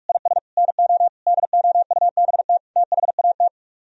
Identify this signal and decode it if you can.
Morse code, heard as a keyed tone, about 49 wpm, 700 Hz. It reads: IS NO DOUBT THAT